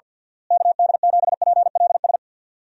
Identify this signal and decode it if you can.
Morse code, heard as a keyed tone, about 51 wpm, 700 Hz. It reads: KD7PLS
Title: USA Call Signs - 6 Characters - Repeated 2x - 50wpm